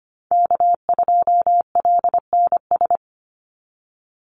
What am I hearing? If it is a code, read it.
Morse code, heard as a keyed tone, about 25 wpm, 700 Hz. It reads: K2LNS
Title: Call Signs - 25wpm